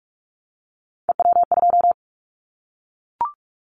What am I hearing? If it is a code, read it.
Morse code, heard as a keyed tone, about 44 wpm, 700 Hz. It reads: EW2
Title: Callsign Prefixes for Top 100 DXCC - 45wpm